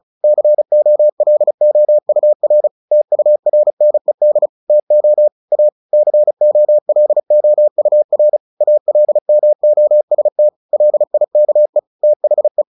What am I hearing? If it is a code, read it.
Morse code, heard as a keyed tone, about 34 wpm, 600 Hz. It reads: COLOUR TURNED TO A COLOUR ALMOST LIKE THE
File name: the old man and the sea / the old man and the sea 1 35wpm